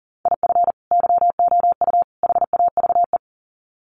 Morse code, heard as a keyed tone, about 40 wpm, 700 Hz. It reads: IF YOU HAVE
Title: Sets of 3 Words 40wpm